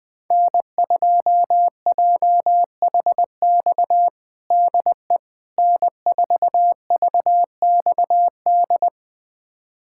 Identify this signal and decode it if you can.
Morse code, heard as a keyed tone, about 20 wpm, 700 Hz. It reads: N2JHX DE N4VXD
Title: USA Callsign DE Callsign - 5 Characters - 20wpm